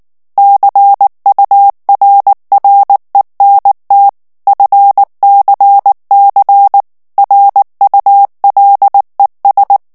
Morse code, heard as a keyed tone, about 19 wpm, 800 Hz. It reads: CURRENT FCC RULES
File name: ANGLAIS texte clair 3 vitesse 18 mots minute